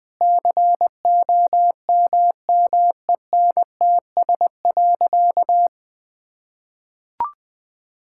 Morse code, heard as a keyed tone, about 20 wpm, 700 Hz. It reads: COMMENTS.
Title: Sentences with 8-Word Phrases - 20wpm